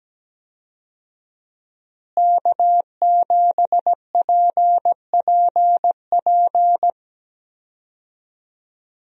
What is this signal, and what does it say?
Morse code, heard as a keyed tone, about 17 wpm, 700 Hz. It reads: K7PPP
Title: USA Call Signs - 5 Characters - Repeated 2x - 17wpm